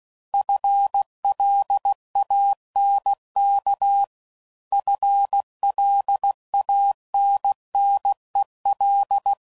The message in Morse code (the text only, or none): FLANK FLANNEL